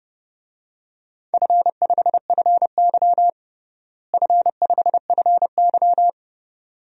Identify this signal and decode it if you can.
Morse code, heard as a keyed tone, about 30 wpm, 700 Hz. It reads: F5FY F5FY
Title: DXCC-100 Call Signs - 4 Characters - Repeated 2x - 30wpm